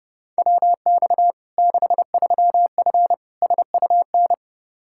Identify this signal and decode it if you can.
Morse code, heard as a keyed tone, about 30 wpm, 700 Hz. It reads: WX 63F SUN
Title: Practice QSO - Exchange 2 - Info - 30wpm